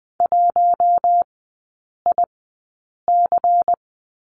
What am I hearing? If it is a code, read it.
Morse code, heard as a keyed tone, about 20 wpm, spaced as for 8 wpm, 700 Hz. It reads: 1IC
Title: 093 Lesson 16e - Letter c - Three Characters 20wpm